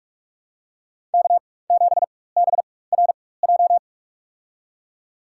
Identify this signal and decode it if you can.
Morse code, heard as a keyed tone, about 46 wpm, spaced as for 21 wpm, 690 Hz. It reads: K7BRJ
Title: USA Call Signs - 5 Characters - 4x Character Spacing - 45wpm